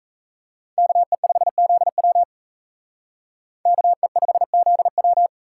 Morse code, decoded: KE5ZW KE5ZW

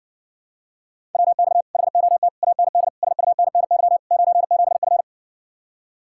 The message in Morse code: WX HOT AND SUNNY 97F

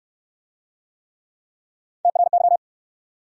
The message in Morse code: TRY

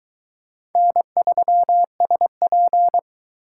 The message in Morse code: N3SP

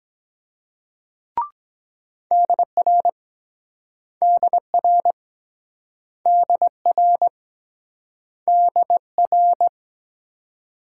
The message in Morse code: DR DR DR DR

